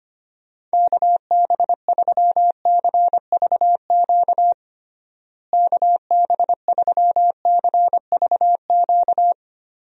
Morse code, heard as a keyed tone, about 25 wpm, 700 Hz. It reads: KB3CVQ KB3CVQ
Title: USA Call Signs - 6 Characters - Repeated 3x - 25wpm